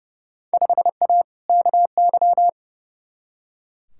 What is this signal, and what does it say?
Morse code, heard as a keyed tone, about 30 wpm, 700 Hz. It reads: 5A KY
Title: Field Day Exchanges - 30wpm